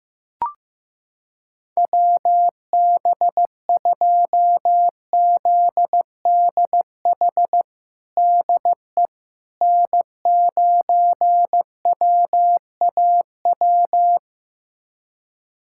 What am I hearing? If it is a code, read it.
Morse code, heard as a keyed tone, about 15 wpm, 700 Hz. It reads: WB2ZDH DE N9WAW
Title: USA Callsign DE Callsign 15wpm